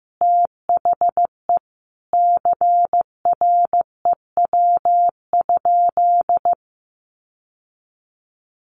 Morse code, heard as a keyed tone, about 14 wpm, 700 Hz. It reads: THE CREW?
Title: Sentences from Top 1500 Words 15wpm